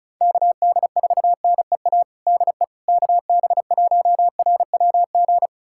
Morse code, heard as a keyed tone, about 36 wpm, 700 Hz. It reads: KD4NEA DE KB1RWG
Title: USA Callsign DE Callsign - 6 Characters - 35wpm